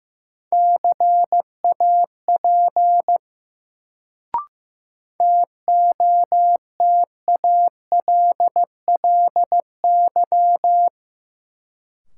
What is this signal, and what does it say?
Morse code, heard as a keyed tone, about 15 wpm, 700 Hz. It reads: CAP TOTALLY
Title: Sentences from Top 1600 Words - New Word Drill - 15wpm